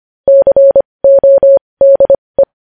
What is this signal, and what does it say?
Morse code, heard as a keyed tone, about 25 wpm, spaced as for 20 wpm, 550 Hz. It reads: CODE
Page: news-headlines-in-morse-code-at-20-wpm